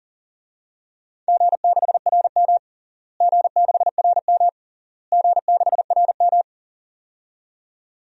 Morse code, heard as a keyed tone, about 40 wpm, 700 Hz. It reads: G6RM G6RM G6RM
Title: DXCC-100 Call Signs - 4 Characters - Repeated 3x - 40wpm